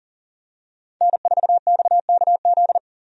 Morse code, heard as a keyed tone, about 40 wpm, 700 Hz. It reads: N4XKZ
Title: USA Call Signs - 5 Characters - Repeated 2x - 40wpm